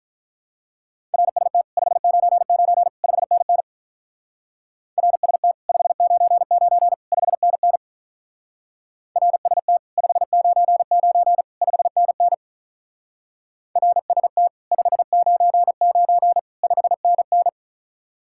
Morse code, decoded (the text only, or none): RST 599 5NN RST 599 5NN RST 599 5NN RST 599 5NN